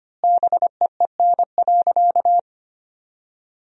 Morse code, heard as a keyed tone, about 25 wpm, 700 Hz. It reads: BEEN.
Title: 4x Word-Spacing - Sentences from Top 300 Words - 25wpm